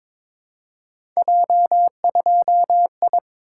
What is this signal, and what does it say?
Morse code, heard as a keyed tone, about 22 wpm, 700 Hz. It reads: J2I
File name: practice / 22wpm / Three Letter-Number Combinations - 22wpm